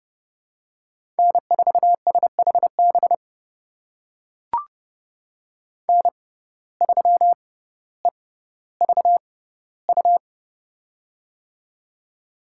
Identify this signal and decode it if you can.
Morse code, heard as a keyed tone, about 30 wpm, spaced as for 15 wpm, 700 Hz. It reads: N4SHB N3EVU
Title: USA Call Signs - 5 Characters - 6x Character Spacing - 30wpm